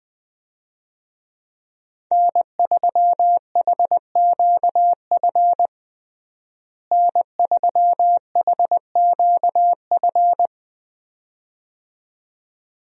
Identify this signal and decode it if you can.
Morse code, heard as a keyed tone, about 20 wpm, 700 Hz. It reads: N3HQF N3HQF
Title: USA Call Signs - 5 Characters - Repeated 2x - 20wpm